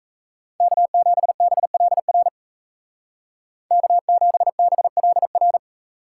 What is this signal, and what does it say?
Morse code, heard as a keyed tone, about 37 wpm, 700 Hz. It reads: K7BLR K7BLR